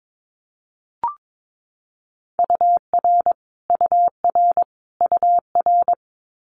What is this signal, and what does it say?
Morse code, heard as a keyed tone, about 22 wpm, 700 Hz. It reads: UR UR UR
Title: Abbreviations Repeated 3x - 22wpm